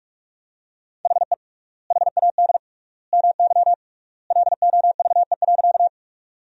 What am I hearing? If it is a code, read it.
Morse code, heard as a keyed tone, about 45 wpm, 700 Hz. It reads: HE HAD MY LOVE.
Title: 3x Word-Spacing - Sentences from Top 400 Words - 45wpm